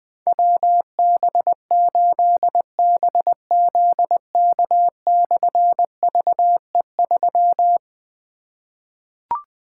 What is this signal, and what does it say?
Morse code, heard as a keyed tone, about 20 wpm, 700 Hz. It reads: WB8BZK/VE3
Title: Difficult Call Signs 20wpm